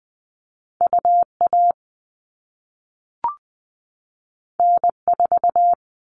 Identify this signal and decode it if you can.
Morse code, heard as a keyed tone, about 20 wpm, 700 Hz. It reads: UA N4
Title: All Cut Numbers - 2-Digit 20wpm